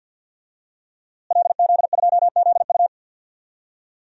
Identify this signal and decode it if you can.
Morse code, heard as a keyed tone, about 52 wpm, 700 Hz. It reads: LZ2ZU
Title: DXCC-100 Call Signs - 5 Characters - Repeated 2x - 50wpm